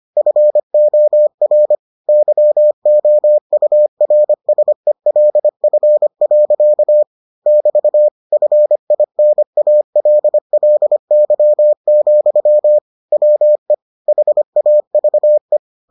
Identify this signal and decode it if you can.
Morse code, heard as a keyed tone, about 25 wpm, 600 Hz. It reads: FOR YOURSELF. <BT> FINALLY, WE HAVE